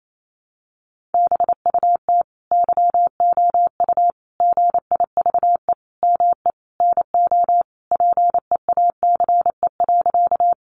Morse code, heard as a keyed tone, about 28 wpm, 700 Hz. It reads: BUT YOU GIVE ME NO PEACE.
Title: Sentences from Top 1500 Words 28wpm